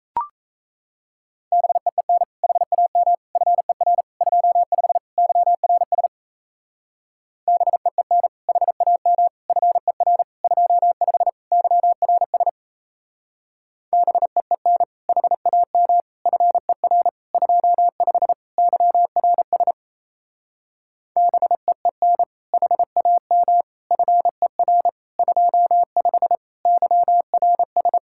Morse code, heard as a keyed tone, about 36 wpm, 700 Hz. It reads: BEEN HAM FER 25 YRS BEEN HAM FER 25 YRS BEEN HAM FER 25 YRS BEEN HAM FER 25 YRS